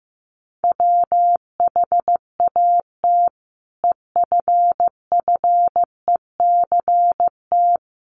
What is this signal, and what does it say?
Morse code, heard as a keyed tone, about 15 wpm, 700 Hz. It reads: WHAT EFFECT
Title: Sentences from Top 1000 Words 15wpm